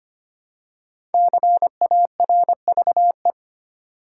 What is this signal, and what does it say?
Morse code, heard as a keyed tone, about 25 wpm, 700 Hz. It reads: CARVE
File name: 5 Letter Words - First 3 Letters Given - 25wpm